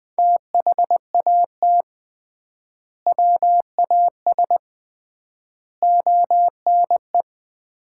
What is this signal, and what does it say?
Morse code, heard as a keyed tone, about 20 wpm, 700 Hz. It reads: THAT WAS ONE
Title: 3x Word-Spacing - Sets of 5 Words - 20wpm